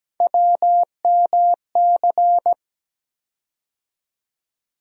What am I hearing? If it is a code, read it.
Morse code, heard as a keyed tone, about 17 wpm, 700 Hz. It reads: WMC
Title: USA Call Sign Postfixes - 3 Letters 17wpm